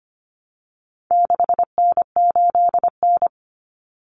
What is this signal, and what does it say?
Morse code, heard as a keyed tone, about 25 wpm, 700 Hz. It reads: 6N8N
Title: Common Cut Numbers - 4-Digit 25wpm